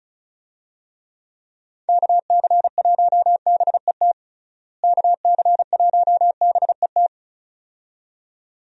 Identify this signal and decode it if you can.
Morse code, heard as a keyed tone, about 34 wpm, 700 Hz. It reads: KC1BET KC1BET
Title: USA Call Signs - 6 Characters - Repeated 2x - 35wpm